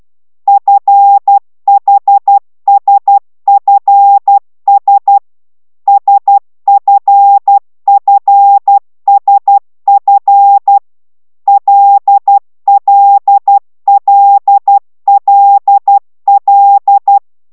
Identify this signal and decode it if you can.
Morse code, heard as a keyed tone, about 12 wpm, 800 Hz. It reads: FHSFS SFFSF LLLLL
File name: taviro tanulas / cw 69 10 10